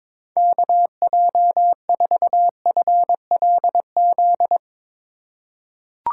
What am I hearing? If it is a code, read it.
Morse code, heard as a keyed tone, about 22 wpm, 700 Hz. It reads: KJ4FLZ
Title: USA Call Signs - 6 Characters - 4x Character Spacing - 22wpm